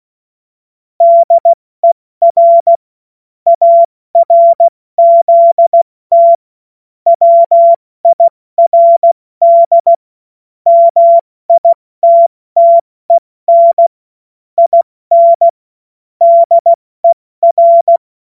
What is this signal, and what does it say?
Morse code, heard as a keyed tone, about 16 wpm, spaced as for 14 wpm, 680 Hz. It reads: DER ARZT WIRD MITTEN IN DER
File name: Morsekurs / Witze / Witz 26 März 2024 70 F80